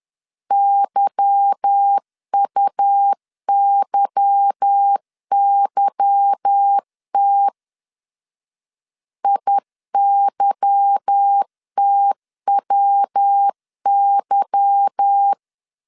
Morse code, heard as a keyed tone, about 10 wpm, 800 Hz. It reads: YUYYT IYTWY
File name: LEZIONI / 8 YJ45 / 8 Y 10wpm 2615